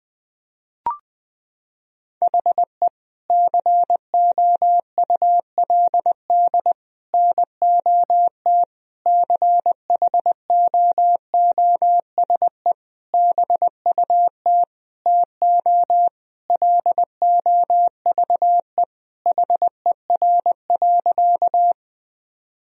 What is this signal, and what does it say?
Morse code, heard as a keyed tone, about 21 wpm, 700 Hz. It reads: HE COULD NOT CHOOSE BUT TO LOVE HER.
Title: Sentences from Top 1000 Words - Review Part 4 - 20wpm